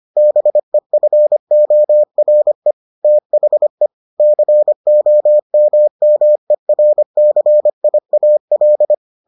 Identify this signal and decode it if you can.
Morse code, heard as a keyed tone, about 25 wpm, 600 Hz. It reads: BEFORE THE COMMERCIAL